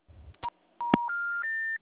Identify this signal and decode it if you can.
Special information tone.
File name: out- -2006- - - .6267